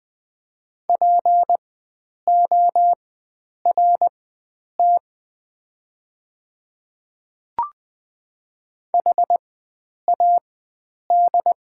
Morse code, heard as a keyed tone, about 20 wpm, spaced as for 9 wpm, 700 Hz. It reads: PORT HAD
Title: 4x Character Spacing - Top 300 Words - 20wpm